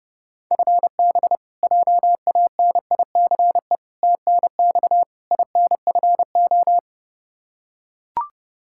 Morse code, heard as a keyed tone, about 30 wpm, 700 Hz. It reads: FB JANICE TNX INFO
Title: K1USN SST QSO Practice 30wpm